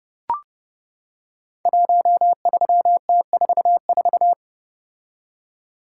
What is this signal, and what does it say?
Morse code, heard as a keyed tone, about 30 wpm, 700 Hz. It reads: 13T44
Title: Code Groups - 5 Letter:Numbers - 30wpm